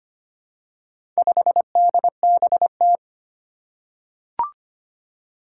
Morse code, heard as a keyed tone, about 25 wpm, 700 Hz. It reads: 5DBT
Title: Four Challenging Letter-Number Combinations 25wpm